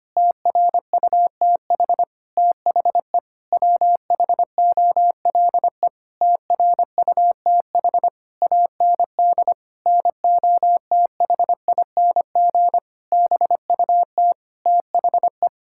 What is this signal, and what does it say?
Morse code, heard as a keyed tone, about 25 wpm, 700 Hz. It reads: TRUTH THE WHOLE TRUTH AND NOTHING BUT THE
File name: practice / 25wpm / Sentences with 9-Word Phrases - 25wpm